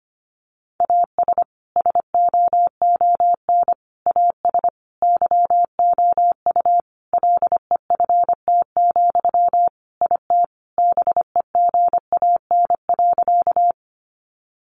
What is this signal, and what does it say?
Morse code, heard as a keyed tone, about 25 wpm, 700 Hz. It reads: AS SOON AS YOU LEFT, IT BEGAN.
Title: Sentences from Top 400 Words 25wpm